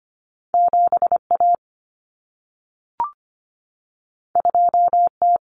Morse code, heard as a keyed tone, about 25 wpm, 700 Hz. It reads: 7A 2T
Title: Common Cut Numbers - 2-Digit 25wpm